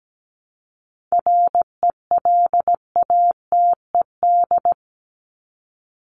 Morse code, heard as a keyed tone, about 17 wpm, 700 Hz. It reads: RELATED
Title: ED Suffix - Top 100 Words with Base Words 17wpm